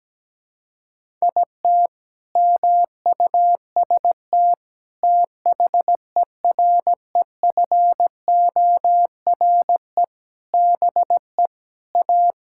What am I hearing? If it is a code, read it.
Morse code, heard as a keyed tone, about 17 wpm, 700 Hz. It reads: IT MUST THEREFORE BE A
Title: Sentences from Top 1000 Words - Review Part 6 - 17wpm